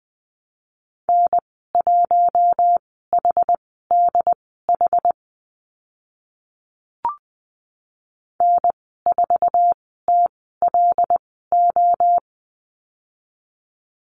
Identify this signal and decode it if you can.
Morse code, heard as a keyed tone, about 20 wpm, spaced as for 14 wpm, 700 Hz. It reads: N1HDH N4TLO
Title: 076 Lesson 13g - Letter L - Call Signs 20wpm